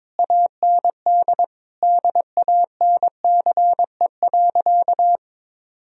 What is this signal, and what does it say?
Morse code, heard as a keyed tone, about 21 wpm, 700 Hz. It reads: AND DANCE.
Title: Sentences from Top 1000 Words - Review Part 4 - 22wpm